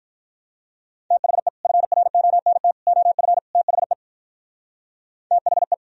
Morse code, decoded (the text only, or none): THE FRONT OF THE THE